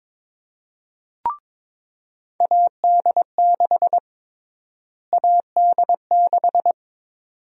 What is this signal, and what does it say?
Morse code, heard as a keyed tone, about 23 wpm, 700 Hz. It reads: AD6 AD6
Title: USA Call Sign Prefixes - Repeated 2x - 22wpm